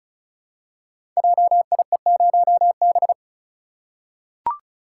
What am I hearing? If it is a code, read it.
Morse code, heard as a keyed tone, about 36 wpm, 690 Hz. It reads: JIE0B
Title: Code Groups - 5 Letter:Numbers - 35wpm